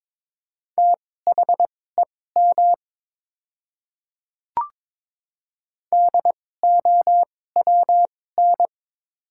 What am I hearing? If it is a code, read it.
Morse code, heard as a keyed tone, about 22 wpm, spaced as for 16 wpm, 690 Hz. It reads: THEM DOWN
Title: 2x Character Spacing - Top 100 Words - 22wpm